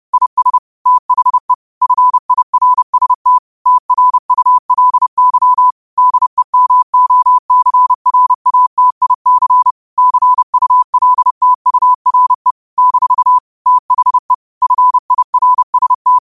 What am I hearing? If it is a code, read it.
Morse code, heard as a keyed tone, about 30 wpm, 1000 Hz. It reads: IS THE FIRST TRULY DEMOCRATIC CULTURE - THE FIRST